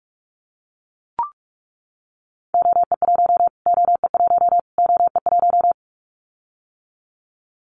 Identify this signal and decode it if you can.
Morse code, heard as a keyed tone, about 44 wpm, 700 Hz. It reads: OE1 OE1 OE1
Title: Callsign Prefixes for Top 25 DXCC - Repeated 3x - 45wpm